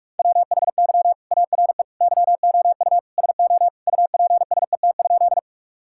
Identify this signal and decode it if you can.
Morse code, heard as a keyed tone, about 41 wpm, 700 Hz. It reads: WHY ARE YOU SO UPSET?
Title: Sentences from Top 1000 Words - Review Part 5 - 45wpm